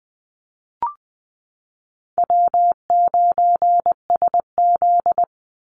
Morse code, heard as a keyed tone, about 20 wpm, 700 Hz. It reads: W9SZ
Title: Call Signs Repeated 3x - 20wpm